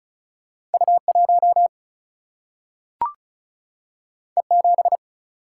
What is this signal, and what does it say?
Morse code, heard as a keyed tone, about 36 wpm, 700 Hz. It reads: U 1 E 7
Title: Single-Letter Single-Number 35wpm